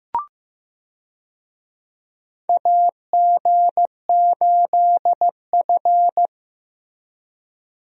Morse code, heard as a keyed tone, about 15 wpm, 700 Hz. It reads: AG8F
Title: USA Call Signs - 4 Characters - Repeated 3x - 15wpm